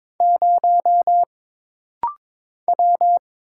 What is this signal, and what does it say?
Morse code, heard as a keyed tone, about 22 wpm, 700 Hz. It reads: 0 W
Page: This rapid-fire timing is geared to push your ICR (Instant Character Recognition) skills.